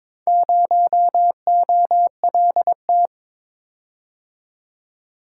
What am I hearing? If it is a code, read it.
Morse code, heard as a keyed tone, about 21 wpm, 690 Hz. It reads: 0OLT
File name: practice / 22wpm / Four Letter-Number Combinations - 22wpm